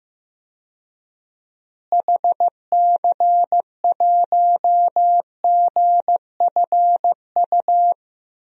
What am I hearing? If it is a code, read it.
Morse code, heard as a keyed tone, about 15 wpm, 700 Hz. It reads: HC1GFU
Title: DXCC-100 Call Signs - 6 Characters - Repeated 3x - 15wpm